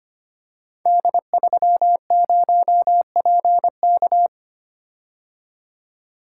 Morse code, heard as a keyed tone, about 25 wpm, 700 Hz. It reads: D30PK
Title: Code Groups - 5 Letter:Numbers - 25wpm